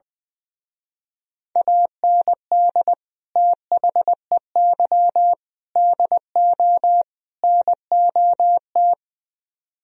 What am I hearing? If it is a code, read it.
Morse code, heard as a keyed tone, about 20 wpm, 700 Hz. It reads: AND THEY DO NOT
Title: Speed-Racing - Sets of 4 Words 20wpm